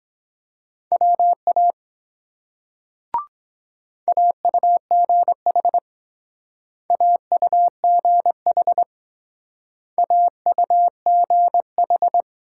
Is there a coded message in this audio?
Yes: Morse code from 0.9 to 12.2 s, heard as a keyed tone, about 22 wpm, 700 Hz. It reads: WA AUGH AUGH AUGH